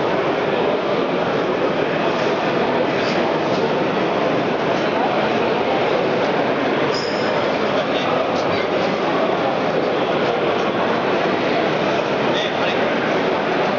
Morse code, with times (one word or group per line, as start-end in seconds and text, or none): none